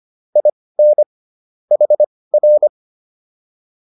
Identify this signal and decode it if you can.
Morse code, heard as a keyed tone, about 25 wpm, spaced as for 18 wpm, 600 Hz. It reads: IN HR
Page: intermediate-practice-files